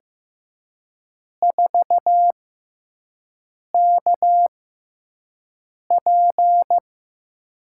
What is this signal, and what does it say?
Morse code, heard as a keyed tone, about 15 wpm, spaced as for 5 wpm, 700 Hz. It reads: 4KP
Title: USA Call Signs - 4 Characters - 6x Character Spacing - 15wpm